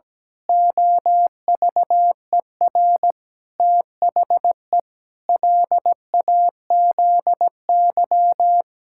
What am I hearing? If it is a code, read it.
Morse code, heard as a keyed tone, about 17 wpm, 700 Hz. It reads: OVER THE LAZY
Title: Pangram Challenge v2 - 17wpm